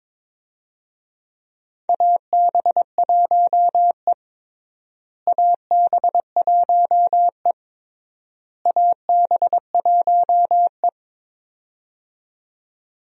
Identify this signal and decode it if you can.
Morse code, heard as a keyed tone, about 22 wpm, 700 Hz. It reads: AB1E AB1E AB1E
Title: USA Call Signs - 4 Characters - Repeated 3x - 22wpm